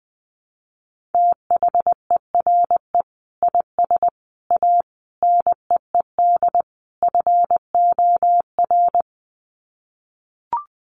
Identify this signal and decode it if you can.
Morse code, heard as a keyed tone, about 20 wpm, 700 Hz. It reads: THERE IS A NEED FOR
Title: Sets of 5 Words Encore2 20wpm